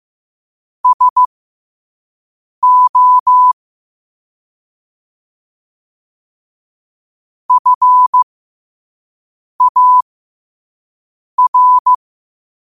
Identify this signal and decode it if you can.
Morse code, heard as a keyed tone, about 15 wpm, spaced as for 5 wpm, 1000 Hz. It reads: SO FAR
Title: 5 WPM morse code quotes for Fri, 01 Mar 2024